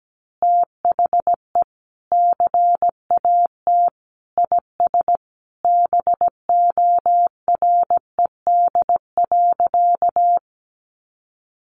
Morse code, heard as a keyed tone, about 17 wpm, 700 Hz. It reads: THE CAT IS BORED.
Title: Easy Phrases 17wpm